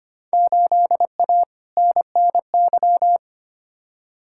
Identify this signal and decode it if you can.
Morse code, heard as a keyed tone, about 25 wpm, 700 Hz. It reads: 8A NNY
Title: Field Day Exchanges Encore - 25wpm